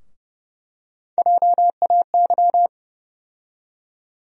Morse code, heard as a keyed tone, about 31 wpm, 700 Hz. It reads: JAY
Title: USA Call Sign Postfixes - 3 Letters 30wpm